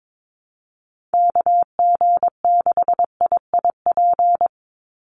Morse code, heard as a keyed tone, about 22 wpm, 700 Hz. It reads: KG6IIP
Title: Call Signs Repeated 2x - 22wpm